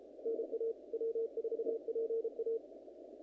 Morse code, heard as a keyed tone, about 32 wpm, 460 Hz. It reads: NA W4PA
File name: contests / naqp-cw-k5tr-op-k5pi- -2000